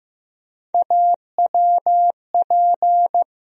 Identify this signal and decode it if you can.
Morse code, heard as a keyed tone, about 15 wpm, 700 Hz. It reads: AWP
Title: USA Call Sign Postfixes - 3 Letters 15wpm